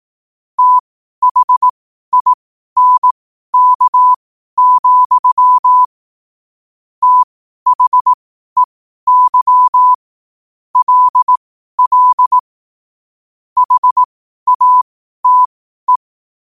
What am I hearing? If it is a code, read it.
Morse code, heard as a keyed tone, about 18 wpm, spaced as for 12 wpm, 1000 Hz. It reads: THINK, THEY LL HATE
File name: qotdfiles / 12wpm